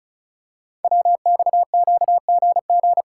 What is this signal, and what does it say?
Morse code, heard as a keyed tone, about 35 wpm, 700 Hz. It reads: WXQGG
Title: Code Groups - 5 Letters - 35wpm